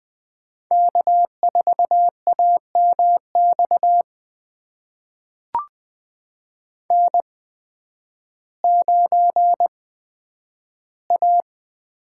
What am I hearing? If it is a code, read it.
Morse code, heard as a keyed tone, about 20 wpm, spaced as for 11 wpm, 700 Hz. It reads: K4AMX N9A
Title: USA Call Signs - 5 Characters - 8x Character Spacing - 20wpm